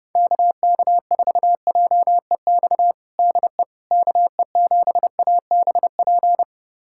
Morse code, heard as a keyed tone, about 30 wpm, 700 Hz. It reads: KK4JEX DE KE7ABP
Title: USA Callsign DE Callsign - 6 Characters - 30wpm